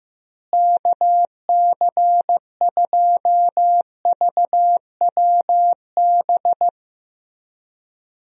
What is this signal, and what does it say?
Morse code, heard as a keyed tone, about 15 wpm, 700 Hz. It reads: KC2VWB
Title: USA Call Signs - 6 Characters - Repeated 3x - 15wpm